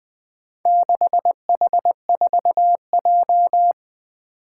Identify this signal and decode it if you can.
Morse code, heard as a keyed tone, about 21 wpm, 700 Hz. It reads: 6H4J
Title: Four Challenging Letter-Number Combinations 20wpm